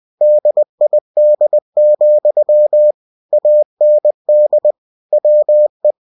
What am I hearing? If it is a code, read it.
Morse code, heard as a keyed tone, about 20 wpm, 600 Hz. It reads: DID, AND WE
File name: cw-books / winnie-the-pooh / winnie-20wpm-page-0107